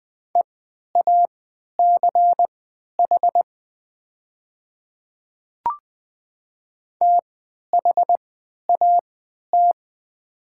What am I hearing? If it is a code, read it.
Morse code, heard as a keyed tone, about 20 wpm, spaced as for 11 wpm, 700 Hz. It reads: EACH THAT
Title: 3x Character Spacing - Top 100 Words - 20wpm